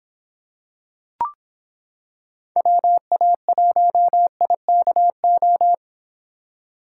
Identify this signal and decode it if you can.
Morse code, heard as a keyed tone, about 26 wpm, 700 Hz. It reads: WA1IKO